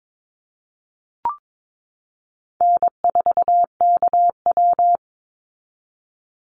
Morse code, heard as a keyed tone, about 22 wpm, 690 Hz. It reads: N4KW
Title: Call Signs Repeated 3x - 22wpm